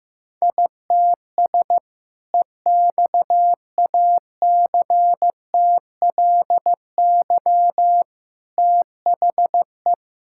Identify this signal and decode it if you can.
Morse code, heard as a keyed tone, about 15 wpm, 700 Hz. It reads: ITS EXACTLY THE
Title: Sentences from Top 1000 Words - Review Part 6 - 15wpm